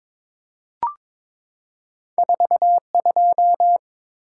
Morse code, heard as a keyed tone, about 22 wpm, 690 Hz. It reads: 4 2
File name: practice / 22wpm / Two Letter-Number Combinations - 22wpm